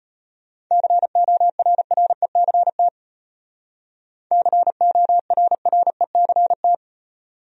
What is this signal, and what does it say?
Morse code, heard as a keyed tone, about 30 wpm, 700 Hz. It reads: CORRECT CORRECT
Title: Speed-Racing Top 500 Words 25wpm